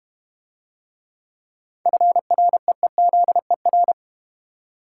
Morse code, heard as a keyed tone, about 32 wpm, 690 Hz. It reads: FREEZER